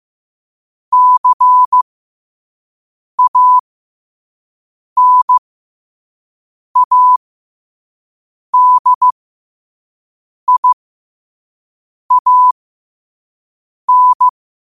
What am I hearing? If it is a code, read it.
Morse code, heard as a keyed tone, about 15 wpm, spaced as for 5 wpm, 1000 Hz. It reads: CANADIAN
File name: qotdfiles / 5wpm